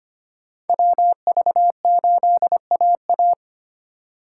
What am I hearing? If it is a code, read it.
Morse code, heard as a keyed tone, about 32 wpm, 700 Hz. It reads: WV8AA